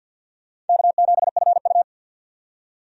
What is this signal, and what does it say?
Morse code, heard as a keyed tone, about 52 wpm, 700 Hz. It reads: K7FU
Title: USA Call Signs - 4 Characters - 50wpm